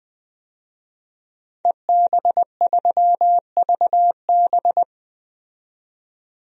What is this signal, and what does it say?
Morse code, heard as a keyed tone, about 20 wpm, 700 Hz. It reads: EB3VB
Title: DXCC-100 Call Signs - 5 Characters - 20wpm